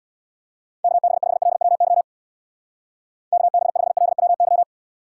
Morse code, heard as a keyed tone, about 61 wpm, 700 Hz. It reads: KB5LWY KB5LWY
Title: Speed Racing - USA Call Signs 50wpm